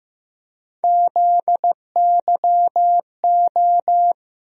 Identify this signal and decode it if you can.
Morse code, heard as a keyed tone, about 15 wpm, 700 Hz. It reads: ZYO
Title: USA Call Sign Postfixes - 3 Letters 15wpm